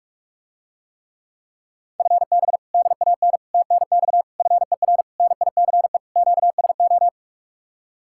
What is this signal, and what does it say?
Morse code, heard as a keyed tone, about 49 wpm, 700 Hz. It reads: FB DAN TNX FER NICE QSO